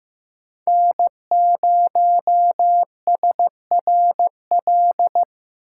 Morse code, heard as a keyed tone, about 15 wpm, 700 Hz. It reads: N0SRL
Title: USA Call Signs - 5 Characters - 4x Character Spacing - 15wpm